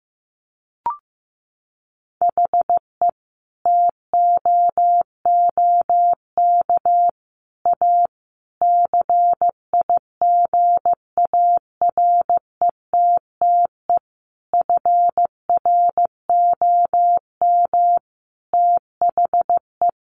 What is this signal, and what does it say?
Morse code, heard as a keyed tone, about 15 wpm, 700 Hz. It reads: HE TOOK A CIGARETTE FROM THE
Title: Sentences from Top 1000 Words 15wpm